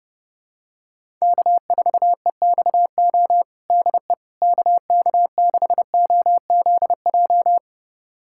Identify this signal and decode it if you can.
Morse code, heard as a keyed tone, about 30 wpm, 700 Hz. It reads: K4EXO DE KK6OZJ
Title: USA Callsign DE Callsign 30wpm